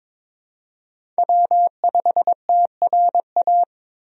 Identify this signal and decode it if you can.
Morse code, heard as a keyed tone, about 21 wpm, 700 Hz. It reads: W5TRA
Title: USA Call Signs - 5 Characters - Repeated 2x - 22wpm